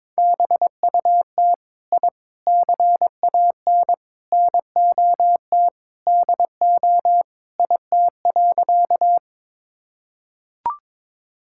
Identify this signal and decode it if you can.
Morse code, heard as a keyed tone, about 22 wpm, 700 Hz. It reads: BUT I CAN NOT DO IT.
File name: practice / 22wpm / Sentences with 4-Word Phrases - 22wpm